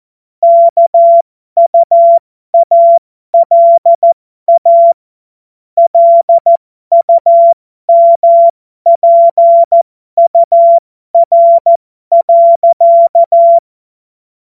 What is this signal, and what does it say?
Morse code, heard as a keyed tone, about 14 wpm, spaced as for 12 wpm, 680 Hz. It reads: KUALA LUMPUR.
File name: Morsekurs / Witze